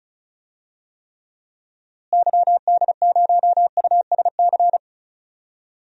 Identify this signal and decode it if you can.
Morse code, heard as a keyed tone, about 36 wpm, 700 Hz. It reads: YD0USC
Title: DXCC-100 Call Signs - 6 Characters - Repeated 2x - 35wpm